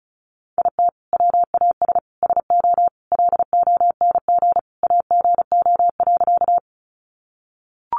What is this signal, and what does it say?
Morse code, heard as a keyed tone, about 34 wpm, 700 Hz. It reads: IT WAS SO LONG AGO.
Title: Sentences from Top 500 Words 35wpm